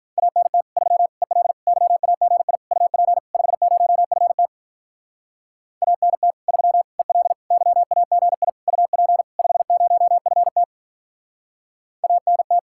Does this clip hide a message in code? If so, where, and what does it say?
Yes: Morse code from 0.2 to 12.6 s, heard as a keyed tone, about 46 wpm, 700 Hz. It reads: ANT 3 EL YAGI UP 50FT ANT 3 EL YAGI UP 50FT ANT